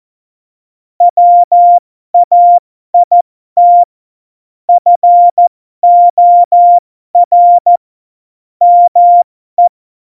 Morse code, heard as a keyed tone, about 14 wpm, spaced as for 12 wpm, 700 Hz. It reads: WAIT FOR ME
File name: Morsekurs / Witze